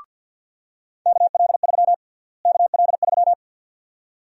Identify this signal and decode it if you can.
Morse code, heard as a keyed tone, about 51 wpm, 700 Hz. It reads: KL3 KL3
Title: USA Call Sign Prefixes - Repeated 2x - 50wpm